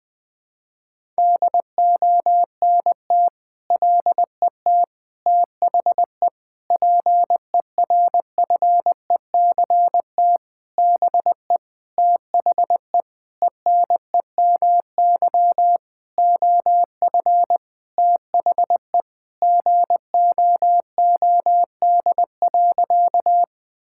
Morse code, heard as a keyed tone, about 20 wpm, 700 Hz. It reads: DONT LET THE PERFECT BE THE ENEMY OF THE GOOD.
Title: Sentences with 9-Word Phrases - Spoken Beginning - 20wpm